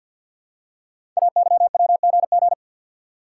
Morse code, heard as a keyed tone, about 48 wpm, 690 Hz. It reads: AYWGG
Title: Code Groups - 5 Letters - 50wpm